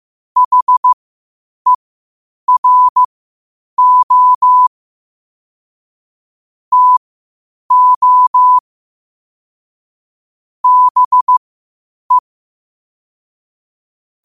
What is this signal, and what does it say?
Morse code, heard as a keyed tone, about 15 wpm, spaced as for 8 wpm, 1000 Hz. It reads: HERO TO BE
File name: qotdfiles / 8wpm